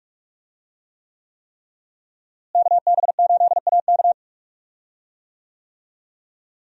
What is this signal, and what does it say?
Morse code, heard as a keyed tone, about 44 wpm, 700 Hz. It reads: KB8AK
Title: USA Call Signs - 5 Characters - Repeated 3x - 45wpm